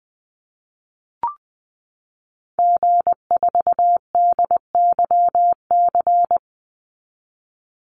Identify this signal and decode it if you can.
Morse code, heard as a keyed tone, about 20 wpm, 700 Hz. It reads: G4DYC
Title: Call Signs Repeated 2x - 20wpm